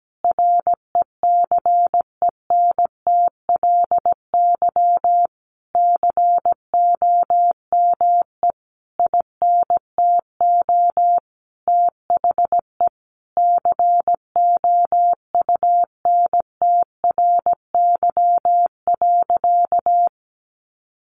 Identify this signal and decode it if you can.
Morse code, heard as a keyed tone, about 17 wpm, 700 Hz. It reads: RECENTLY COME INTO THE COUNTRY.
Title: Sentences from Top 1200 Words 17wpm